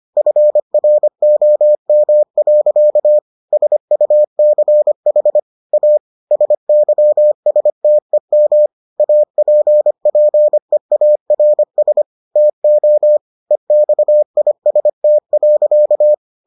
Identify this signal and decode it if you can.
Morse code, heard as a keyed tone, about 25 wpm, 600 Hz. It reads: FROM. SUCH A SYSTEM APPEARS TO EXIST.